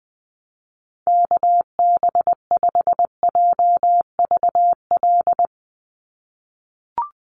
Morse code, heard as a keyed tone, about 20 wpm, 700 Hz. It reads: KB5JVL
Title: USA Call Signs 20wpm